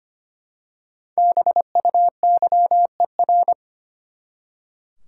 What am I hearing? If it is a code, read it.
Morse code, heard as a keyed tone, about 25 wpm, 700 Hz. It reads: BUYER
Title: Five Letter Words - 25wpm